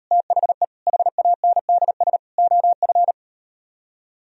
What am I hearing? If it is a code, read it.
Morse code, heard as a keyed tone, about 38 wpm, 700 Hz. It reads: THE HANDS OF